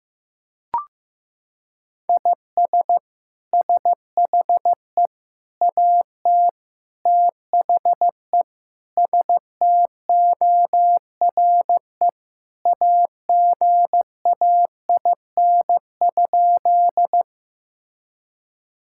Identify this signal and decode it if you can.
Morse code, heard as a keyed tone, about 15 wpm, 700 Hz. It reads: IS SHE AT THE STORE AGAIN?
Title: Sentences from Top 1000 Words - Review Part 5 - 15wpm